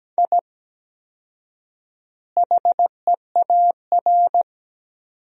Morse code, heard as a keyed tone, about 17 wpm, 700 Hz. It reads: I HEAR
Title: 4x Word-Spacing - Sentences from Top 500 Words - 17wpm